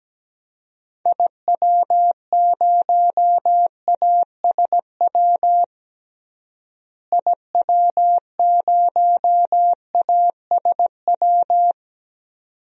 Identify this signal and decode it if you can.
Morse code, heard as a keyed tone, about 17 wpm, 700 Hz. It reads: IW0ASW IW0ASW
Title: DXCC-100 Call Signs - 6 Characters - Repeated 2x - 17wpm